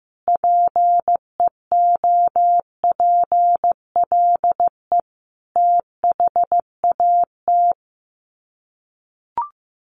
Morse code, heard as a keyed tone, about 15 wpm, 700 Hz. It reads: PEOPLE THAT
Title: Sets of 5 Words Encore2 15wpm